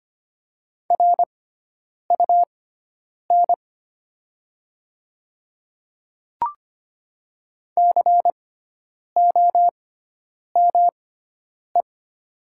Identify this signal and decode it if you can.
Morse code, heard as a keyed tone, about 25 wpm, spaced as for 9 wpm, 700 Hz. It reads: RUN COME
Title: 6x Character Spacing - Top 300 Words - 25wpm